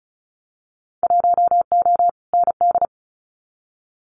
Morse code, heard as a keyed tone, about 34 wpm, 700 Hz. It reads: 1O ND
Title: Winter Field Day Exchanges 35wpm